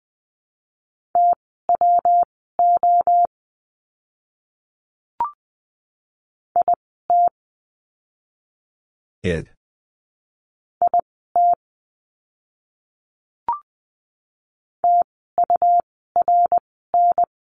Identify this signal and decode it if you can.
Morse code, heard as a keyed tone, about 20 wpm, spaced as for 14 wpm, 700 Hz. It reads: TWO IT IT TURN
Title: 192 Lesson 31f - Letter v - Words 20wpm